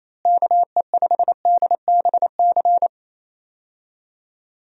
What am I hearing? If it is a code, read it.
Morse code, heard as a keyed tone, about 27 wpm, 700 Hz. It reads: KE5DBC
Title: USA Call Signs - 6 Characters - Repeated 2x - 28wpm